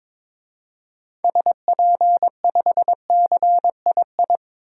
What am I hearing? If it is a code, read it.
Morse code, heard as a keyed tone, about 22 wpm, 700 Hz. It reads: SP5CII
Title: DXCC-100 Call Signs - 6 Characters - 22wpm